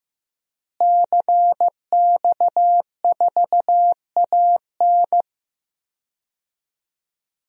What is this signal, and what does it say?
Morse code, heard as a keyed tone, about 15 wpm, 700 Hz. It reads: CX4AN
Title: DXCC-100 Call Signs - 5 Characters - Repeated 2x - 15wpm